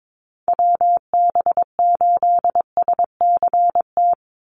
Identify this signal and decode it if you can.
Morse code, heard as a keyed tone, about 22 wpm, 700 Hz. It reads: WB8SCT
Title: Call Signs - 22wpm